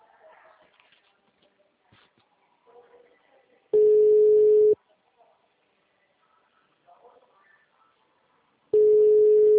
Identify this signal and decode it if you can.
Ringback tone (Europe, 425 Hz), with speech.